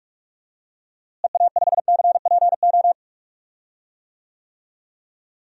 Morse code, heard as a keyed tone, about 47 wpm, 700 Hz. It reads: EA5CPO
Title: DXCC-100 Call Signs - 6 Characters - Repeated 2x - 45wpm